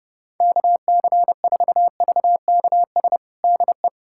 Morse code, heard as a keyed tone, about 32 wpm, 700 Hz. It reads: KC4VKS DE
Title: USA Callsign DE Callsign 30wpm